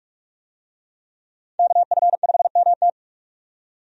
Morse code, heard as a keyed tone, about 46 wpm, 700 Hz. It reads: KF5MT
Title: USA Call Signs - 5 Characters - 45wpm